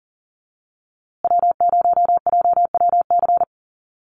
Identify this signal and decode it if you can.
Morse code, heard as a keyed tone, about 40 wpm, 700 Hz. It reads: W0JWC
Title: USA Call Signs Repeated 2x 40wpm